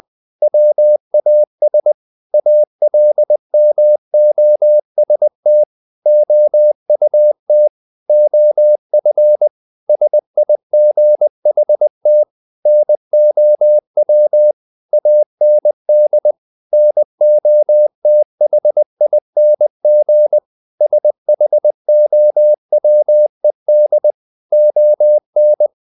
Morse code, heard as a keyed tone, about 20 wpm, 600 Hz. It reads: WAS ALMOST OUT OF SIGHT NOW AND NOTHING SHOWED ON
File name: the old man and the sea / the old man and the sea part 1 20wpm